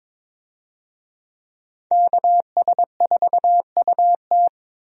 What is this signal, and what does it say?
Morse code, heard as a keyed tone, about 22 wpm, 700 Hz. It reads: KS4UT
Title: USA Call Signs - 5 Characters - 22wpm